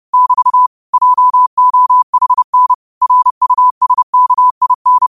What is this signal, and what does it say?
Morse code, heard as a keyed tone, about 30 wpm, 1000 Hz. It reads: <BT> JOHN RUSKIN
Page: New quotes every day in morse code at 30 Words per minute.